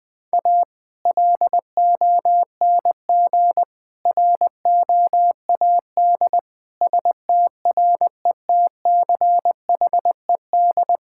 Morse code, heard as a keyed tone, about 20 wpm, 700 Hz. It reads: A LONG ROAD STRETCHED
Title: Sentences with 2-Word Phrases - 20wpm